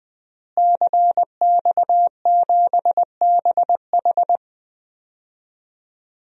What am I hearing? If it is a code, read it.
Morse code, heard as a keyed tone, about 20 wpm, 700 Hz. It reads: CX7BH
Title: DXCC-100 Call Signs - 5 Characters - Repeated 2x - 20wpm